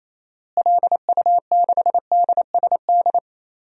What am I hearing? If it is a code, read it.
Morse code, heard as a keyed tone, about 27 wpm, 700 Hz. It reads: LU6DSD
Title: DXCC-100 Call Signs - 6 Characters - Repeated 3x - 28wpm